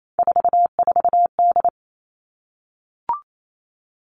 Morse code, heard as a keyed tone, about 28 wpm, 700 Hz. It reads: 44D
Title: All Cut Numbers - 3-Digit 28wpm